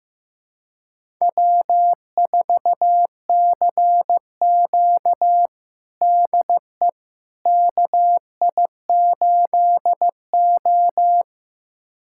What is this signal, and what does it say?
Morse code, heard as a keyed tone, about 15 wpm, 700 Hz. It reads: W4CQ DE KI8O
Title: USA Callsign DE Callsign - 4 Characters - 15wpm